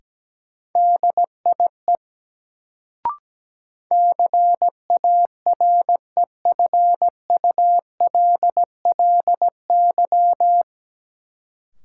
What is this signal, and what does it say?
Morse code, heard as a keyed tone, about 17 wpm, 700 Hz. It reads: DIE CAREFULLY
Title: Sentences from Top 1300 Words - New Word Drill - 17wpm